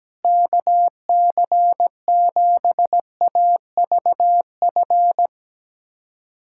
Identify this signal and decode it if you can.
Morse code, heard as a keyed tone, about 17 wpm, 700 Hz. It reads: KC7AVF
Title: USA Callsign DE Callsign - 6 Characters - 17wpm